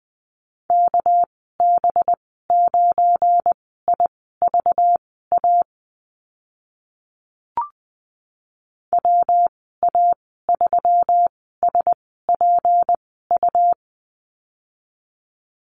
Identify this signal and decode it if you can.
Morse code, heard as a keyed tone, about 20 wpm, spaced as for 14 wpm, 700 Hz. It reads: KB9IVA WA3SPU
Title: 238 Lesson 38g - Letter q - Call Signs 20wpm